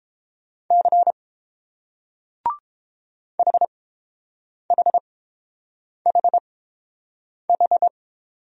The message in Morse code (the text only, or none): C HHHH